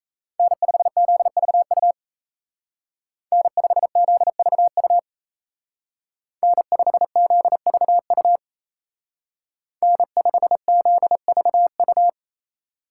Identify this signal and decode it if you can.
Morse code, heard as a keyed tone, about 36 wpm, 700 Hz. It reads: N5ZVU N5ZVU N5ZVU N5ZVU